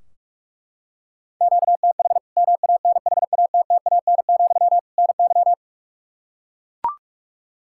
Morse code, heard as a keyed tone, about 41 wpm, 700 Hz. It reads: QTH MANHATTAN, NY
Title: Practice QSO - Exchange 1 - Beginning - 45wpm